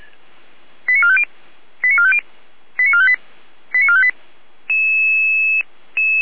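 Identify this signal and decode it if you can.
Five-tone selective call (ZVEI), heard as quick runs of five tones, tones of about 70 ms. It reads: 89450 89450 89458 89458
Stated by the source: ZVEI five-tone call 89325 received by SDR, 455 channel.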